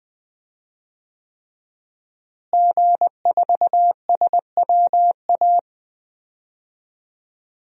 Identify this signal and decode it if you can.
Morse code, heard as a keyed tone, about 20 wpm, 700 Hz. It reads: G4SWA
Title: DXCC-100 Call Signs - 5 Characters - Repeated 3x - 20wpm